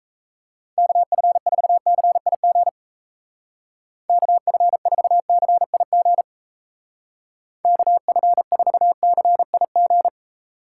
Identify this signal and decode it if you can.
Morse code, heard as a keyed tone, about 33 wpm, 700 Hz. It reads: KF4CIG KF4CIG KF4CIG
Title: Speed Racing - USA 2x3 Call Signs 28wpm